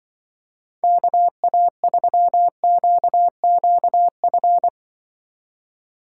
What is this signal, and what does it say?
Morse code, heard as a keyed tone, about 23 wpm, 700 Hz. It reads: KA3QQF
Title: Speed Racing - USA 2x3 Call Signs 20wpm